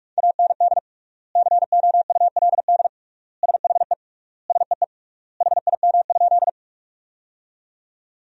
Morse code, heard as a keyed tone, about 46 wpm, 700 Hz. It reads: AND COULD SHE SEE HIM?
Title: 3x Word-Spacing - Sentences from Top 100 Words - 45wpm